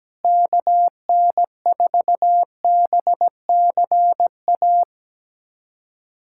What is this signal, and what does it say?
Morse code, heard as a keyed tone, about 17 wpm, 700 Hz. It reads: KN4BCA
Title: USA Callsign DE Callsign - 6 Characters - 17wpm